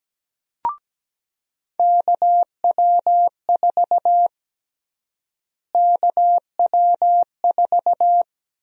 Morse code, heard as a keyed tone, about 17 wpm, 700 Hz. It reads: KW4 KW4
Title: USA Call Sign Prefixes - Repeated 2x - 17wpm